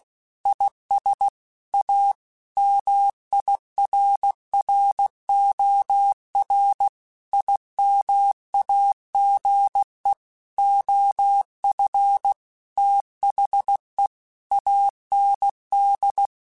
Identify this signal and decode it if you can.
Morse code, heard as a keyed tone, about 16 wpm, 800 Hz. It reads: IS A MIRROR IMAGE OF THE AND
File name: Technical Subjects / Technical Subjects-mp3s / OR Gate @15